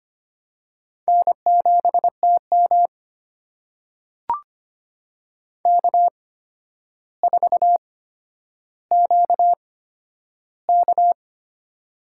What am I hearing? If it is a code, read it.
Morse code, heard as a keyed tone, about 25 wpm, spaced as for 11 wpm, 700 Hz. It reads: N7TM K4QK
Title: USA Call Signs - 4 Characters - 8x Character Spacing - 25wpm